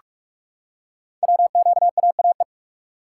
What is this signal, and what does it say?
Morse code, heard as a keyed tone, about 44 wpm, 700 Hz. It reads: WQAAE
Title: Code Groups - 5 Letter:Numbers - 45wpm